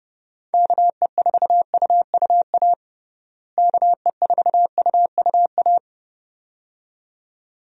Morse code, heard as a keyed tone, about 30 wpm, 700 Hz. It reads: KE4UUA KE4UUA
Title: USA Call Signs - 6 Characters - Repeated 2x - 30wpm